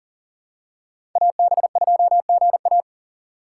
Morse code, heard as a keyed tone, about 40 wpm, 700 Hz. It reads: AB2GA
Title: USA Call Signs - 5 Characters - 2x Character Spacing - 40wpm